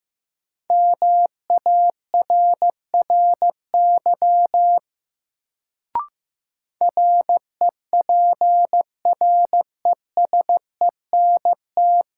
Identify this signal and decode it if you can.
Morse code, heard as a keyed tone, about 15 wpm, 700 Hz. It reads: MARRY REPRESENT
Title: Sentences from Top 1200 Words - New Word Drill Alt - 15wpm